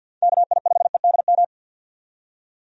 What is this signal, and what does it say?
Morse code, heard as a keyed tone, about 52 wpm, 700 Hz. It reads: KI5EDM
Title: USA Call Signs - 6 Characters - 50wpm